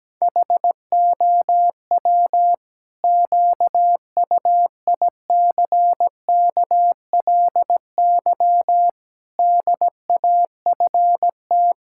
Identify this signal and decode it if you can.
Morse code, heard as a keyed tone, about 17 wpm, 700 Hz. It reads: HOW QUICKLY DAFT
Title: Pangram Challenge v2 - 17wpm